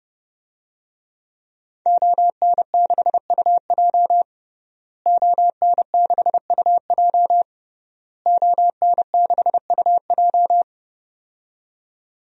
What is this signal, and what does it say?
Morse code, heard as a keyed tone, about 32 wpm, 700 Hz. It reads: ON6UJ ON6UJ ON6UJ
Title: DXCC-100 Call Signs - 5 Characters - Repeated 3x - 30wpm